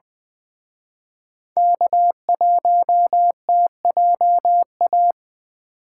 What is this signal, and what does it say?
Morse code, heard as a keyed tone, about 23 wpm, 700 Hz. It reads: K1TJA
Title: Speed Racing - USA 1x3 Call Signs 20wpm